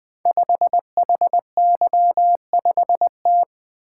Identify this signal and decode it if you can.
Morse code, heard as a keyed tone, about 20 wpm, 700 Hz. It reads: 5HY5T
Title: Code Groups - 5 Letter:Numbers - 20wpm